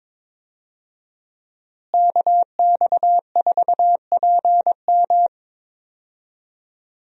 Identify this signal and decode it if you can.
Morse code, heard as a keyed tone, about 22 wpm, 700 Hz. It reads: KX4PM
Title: USA Call Signs - 5 Characters - Repeated 3x - 22wpm